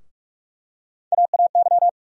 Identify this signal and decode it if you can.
Morse code, heard as a keyed tone, about 47 wpm, 700 Hz. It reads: AAY
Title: USA Call Sign Postfixes - 3 Letters 45wpm